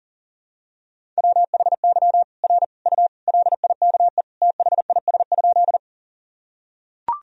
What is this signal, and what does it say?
Morse code, heard as a keyed tone, about 40 wpm, 700 Hz. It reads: WHY R U LIKE THIS?
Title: Sentences from Top 200 Words and Abbreviations 40wpm